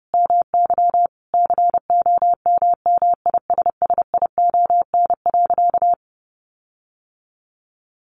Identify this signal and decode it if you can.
Morse code, heard as a keyed tone, about 30 wpm, 700 Hz. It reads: MY COMMISSION.
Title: Sentences from Top 1400 Words 30wpm